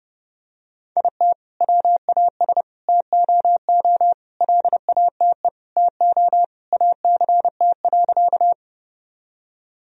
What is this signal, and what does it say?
Morse code, heard as a keyed tone, about 30 wpm, 700 Hz. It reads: IT WAS TOO LATE TO ACT.
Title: Sentences from Top 1000 Words - Review Part 1 - 30wpm